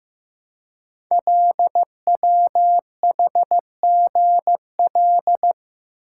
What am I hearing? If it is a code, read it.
Morse code, heard as a keyed tone, about 15 wpm, 700 Hz. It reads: LWHGL
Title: Code Groups - 5 Letter:Numbers - 15wpm